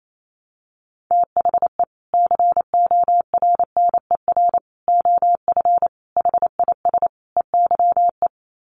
Morse code, heard as a keyed tone, about 28 wpm, 700 Hz. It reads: THE CORNER OF HIS EYE
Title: Sets of 5 Words Encore2 28wpm